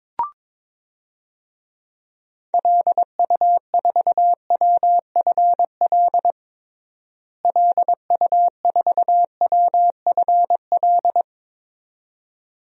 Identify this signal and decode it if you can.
Morse code, heard as a keyed tone, about 22 wpm, 700 Hz. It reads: LU4WFL LU4WFL
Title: DXCC-100 Call Signs - 6 Characters - Repeated 2x - 22wpm